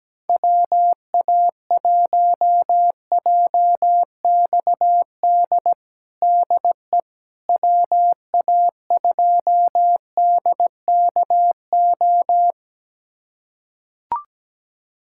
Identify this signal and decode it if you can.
Morse code, heard as a keyed tone, about 17 wpm, 700 Hz. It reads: WA1JXD DE WA2DKO
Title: USA Callsign DE Callsign - 6 Characters - 17wpm